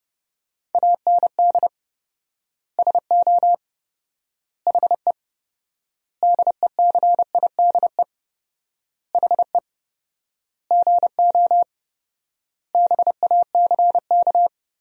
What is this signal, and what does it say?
Morse code, heard as a keyed tone, about 30 wpm, 700 Hz. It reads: AND SO HE DECIDE HE GO BACK
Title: 4x Word-Spacing - Sentences from Top 500 Words - 30wpm